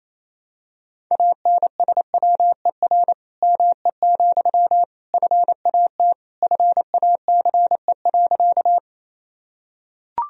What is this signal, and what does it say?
Morse code, heard as a keyed tone, about 28 wpm, 700 Hz. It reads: ANSWER ME, FAT FACE.
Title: Sentences from Top 1000 Words - Review Part 4 - 28wpm